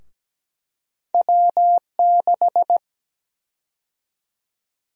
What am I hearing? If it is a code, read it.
Morse code, heard as a keyed tone, about 17 wpm, 700 Hz. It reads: W6
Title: USA Call Sign Prefixes - 17wpm